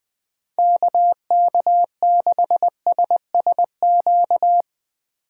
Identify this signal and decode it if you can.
Morse code, heard as a keyed tone, about 21 wpm, 700 Hz. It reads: KK6SSQ
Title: USA Call Signs - 6 Characters - 3x Character Spacing - 20wpm